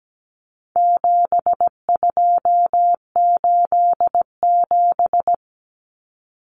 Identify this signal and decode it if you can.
Morse code, heard as a keyed tone, about 17 wpm, 700 Hz. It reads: 7287
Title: Common Cut Numbers - 4-Digit 17wpm